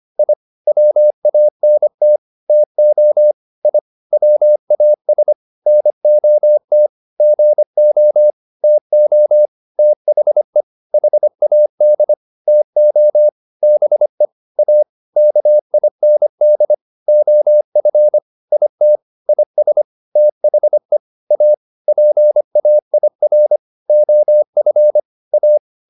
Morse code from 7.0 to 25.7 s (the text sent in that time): GO TO THE HAD TO BE A KIND OF IT IS THE A PAIR OF A